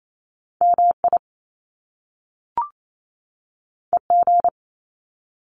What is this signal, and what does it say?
Morse code, heard as a keyed tone, about 28 wpm, 700 Hz. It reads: MI EG
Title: Top Two Letter Combinations in Words 28wpm